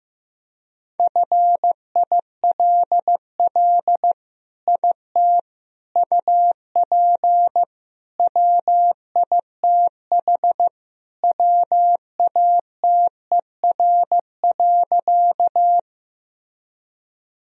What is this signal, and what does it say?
Morse code, heard as a keyed tone, about 16 wpm, 700 Hz. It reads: FILL IT UP WITH WATER.
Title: Sentences from Top 1000 Words - Review Part 3 - 15wpm